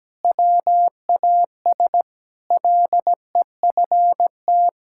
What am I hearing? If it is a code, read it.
Morse code, heard as a keyed tone, about 17 wpm, 700 Hz. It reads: WAS LEFT
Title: Sentences from Top 1000 Words - Review Part 5 - 17wpm